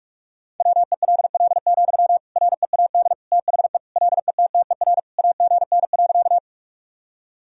Morse code, heard as a keyed tone, about 41 wpm, 700 Hz. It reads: WELL, READ THE LETTER AGN.
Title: Sentences from Top 300 Words and Abbreviations 45wpm